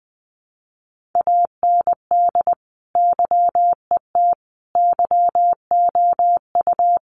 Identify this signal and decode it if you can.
Morse code, heard as a keyed tone, about 20 wpm, 700 Hz. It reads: AND YET YOU
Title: Sentences from Top 1300 Words 20wpm